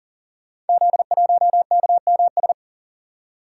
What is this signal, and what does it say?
Morse code, heard as a keyed tone, about 42 wpm, 690 Hz. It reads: Z1KMS
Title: Code Groups - 5 Letter:Numbers - 40wpm